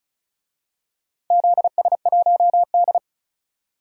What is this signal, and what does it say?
Morse code, heard as a keyed tone, about 36 wpm, 700 Hz. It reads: ZS1D
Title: DXCC-100 Call Signs - 4 Characters - 35wpm